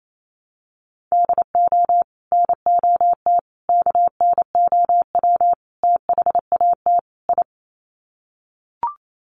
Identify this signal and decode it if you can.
Morse code, heard as a keyed tone, about 28 wpm, 700 Hz. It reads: DO NOT KNOW THAT I
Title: Sets of 5 Words Encore2 28wpm